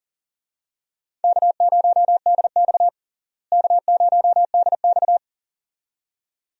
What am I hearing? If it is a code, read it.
Morse code, heard as a keyed tone, about 40 wpm, 700 Hz. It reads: K0DX K0DX
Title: USA Call Signs - 4 Characters - Repeated 2x - 40wpm